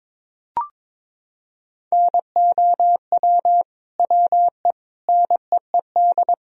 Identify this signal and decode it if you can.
Morse code, heard as a keyed tone, about 22 wpm, 700 Hz. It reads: NOW WE NEED
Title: Sentences from Top 1000 Words - Review Part 4 - 22wpm